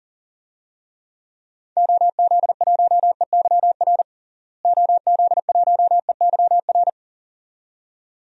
Morse code, heard as a keyed tone, about 37 wpm, 700 Hz. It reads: OZ1EYR OZ1EYR
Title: DXCC-100 Call Signs - 6 Characters - Repeated 2x - 40wpm